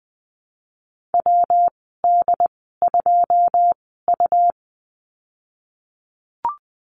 Morse code, heard as a keyed tone, about 20 wpm, spaced as for 15 wpm, 700 Hz. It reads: WD2U
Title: 154 Lesson 25g - Letter p - Call Signs 20wpm